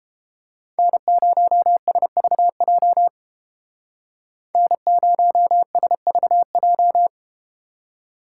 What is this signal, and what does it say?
Morse code, heard as a keyed tone, about 32 wpm, 700 Hz. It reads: N0SVJ N0SVJ